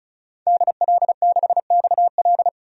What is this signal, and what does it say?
Morse code, heard as a keyed tone, about 34 wpm, 700 Hz. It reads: DL6XL
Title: DXCC-100 Call Signs - 5 Characters - 35wpm